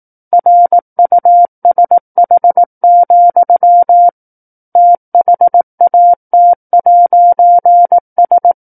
Morse code, heard as a keyed tone, about 18 wpm, 700 Hz. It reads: RUSH, THAT'S
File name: 18 WPM